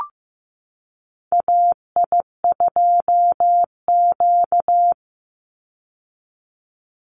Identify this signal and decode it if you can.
Morse code, heard as a keyed tone, about 15 wpm, 700 Hz. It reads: AI2Q
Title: Call Signs Repeated 3x - 15wpm